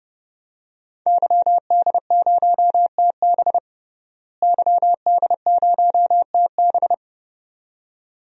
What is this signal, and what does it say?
Morse code, heard as a keyed tone, about 29 wpm, 700 Hz. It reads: YD0TB YD0TB
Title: DXCC-100 Call Signs - 5 Characters - Repeated 2x - 30wpm